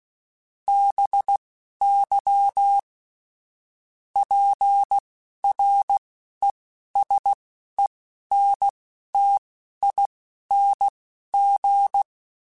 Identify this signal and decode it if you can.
Morse code, heard as a keyed tone, about 16 wpm, spaced as for 11 wpm, 800 Hz. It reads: BY PRESENTING
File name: Technical Subjects / Technical Subjects-mp3s / XC @10